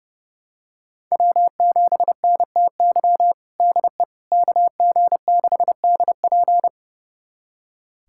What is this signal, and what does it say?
Morse code, heard as a keyed tone, about 31 wpm, 700 Hz. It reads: W7NTY DE KG6DP
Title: USA Callsign DE Callsign - 5 Characters - 30wpm